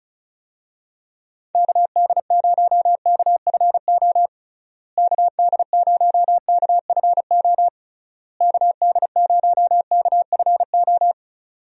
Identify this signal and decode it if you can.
Morse code, heard as a keyed tone, about 34 wpm, 700 Hz. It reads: KD0KFO KD0KFO KD0KFO
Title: USA Call Signs - 6 Characters - Repeated 3x - 35wpm